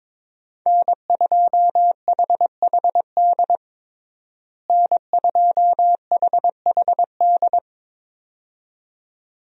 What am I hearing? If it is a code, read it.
Morse code, heard as a keyed tone, about 22 wpm, 700 Hz. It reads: N2HHD N2HHD
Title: USA Call Signs - 5 Characters - Repeated 2x - 22wpm